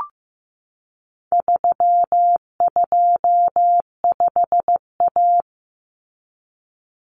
Morse code, heard as a keyed tone, about 15 wpm, 700 Hz. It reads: 325A
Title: Common Cut Numbers - 4-Digit 15wpm